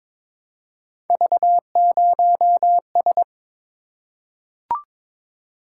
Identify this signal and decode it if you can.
Morse code, heard as a keyed tone, about 22 wpm, 700 Hz. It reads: V0S
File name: practice / 22wpm / Three Letter-Number Combinations - 22wpm